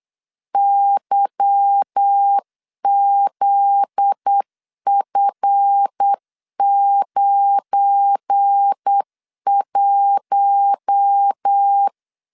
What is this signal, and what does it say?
Morse code, heard as a keyed tone, about 8 wpm, 800 Hz. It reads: YZF91
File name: CW / DET / 1 9 1500C S A 8wpm 5637